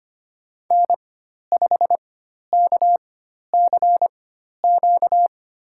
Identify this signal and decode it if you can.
Morse code, heard as a keyed tone, about 25 wpm, spaced as for 12 wpm, 700 Hz. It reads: N5KCQ
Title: USA Call Signs - 5 Characters - 4x Character Spacing - 25wpm